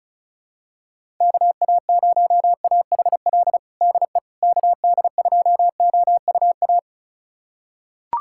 Morse code, heard as a keyed tone, about 36 wpm, 700 Hz. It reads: KA0AHL DE KD2OUA
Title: USA Callsign DE Callsign 35wpm